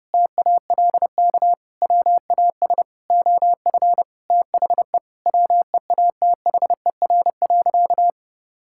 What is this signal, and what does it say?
Morse code, heard as a keyed tone, about 30 wpm, 700 Hz. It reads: TALK WAS OF THE WEATHER.
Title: Sentences from Top 1000 Words - Review Part 5 - 30wpm